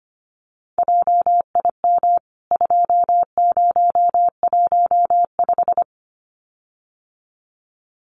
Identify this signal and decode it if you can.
Morse code, heard as a keyed tone, about 25 wpm, 690 Hz. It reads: JIM 2015
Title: CWT Exchange 25wpm